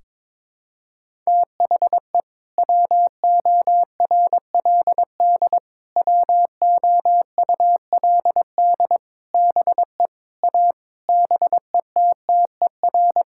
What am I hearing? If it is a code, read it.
Morse code, heard as a keyed tone, about 22 wpm, 700 Hz. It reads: THE WORLD WOULD BE A BETTER
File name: practice / 22wpm / Sets of 8 Words - 22wpm